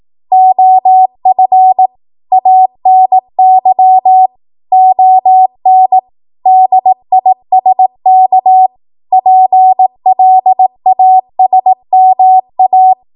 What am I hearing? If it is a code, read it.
Morse code, heard as a keyed tone, about 18 wpm, 750 Hz. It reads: OF ANY ON DISK PLASMA
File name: Morse / 18WPM